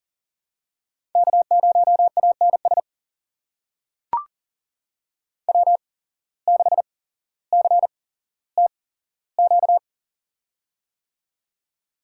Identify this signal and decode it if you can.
Morse code, heard as a keyed tone, about 40 wpm, spaced as for 17 wpm, 700 Hz. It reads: K0ANS W6CTQ
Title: USA Call Signs - 5 Characters - 8x Character Spacing - 40wpm